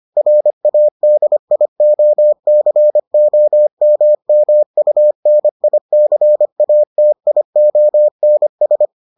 Morse code, heard as a keyed tone, about 25 wpm, 600 Hz. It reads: RADIOCOMMUNICATIONS